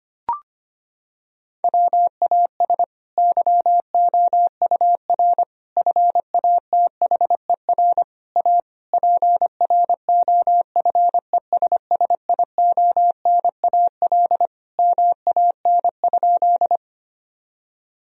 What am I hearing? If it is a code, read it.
Morse code, heard as a keyed tone, about 25 wpm, 700 Hz. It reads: WAS YOUR FATHER A PROFESSIONAL MAN?
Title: Sentences from Top 1000 Words - Review Part 4 - 25wpm